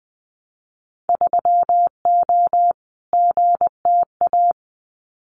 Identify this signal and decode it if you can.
Morse code, heard as a keyed tone, about 20 wpm, 700 Hz. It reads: 3O GTA
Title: Winter Field Day Exchanges Encore 20wpm